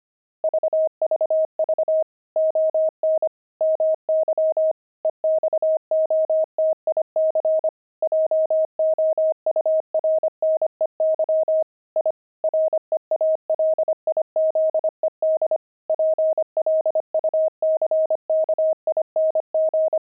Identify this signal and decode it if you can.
Morse code, heard as a keyed tone, about 25 wpm, 630 Hz. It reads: VVV ON MY EXOTIC JOURNEY I REALIZED PLUCKING
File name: pangram-fs-25-cs-25-journey